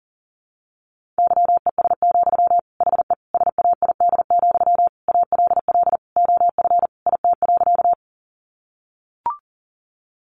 Morse code, heard as a keyed tone, about 40 wpm, 700 Hz. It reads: YES, HE SAID, ALL OF IT.
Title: Sentences from Top 500 Words 40wpm